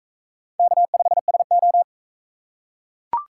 Morse code, heard as a keyed tone, about 41 wpm, 700 Hz. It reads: K5SO